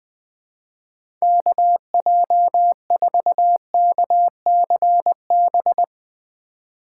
Morse code, heard as a keyed tone, about 21 wpm, 700 Hz. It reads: KJ4KCB
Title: USA Call Signs - 6 Characters - 2x Character Spacing - 20wpm